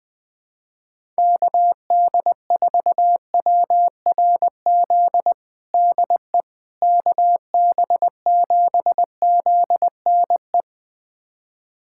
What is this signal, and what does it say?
Morse code, heard as a keyed tone, about 20 wpm, 700 Hz. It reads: KD4WRZ DE KB7ZNE
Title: USA Callsign DE Callsign - 6 Characters - 20wpm